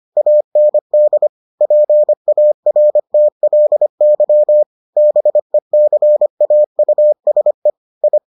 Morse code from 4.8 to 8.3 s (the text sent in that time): BECAUSE I